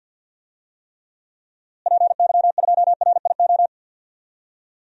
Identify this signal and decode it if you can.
Morse code, heard as a keyed tone, about 51 wpm, 700 Hz. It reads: PY2RIO
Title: DXCC-100 Call Signs - 6 Characters - Repeated 3x - 50wpm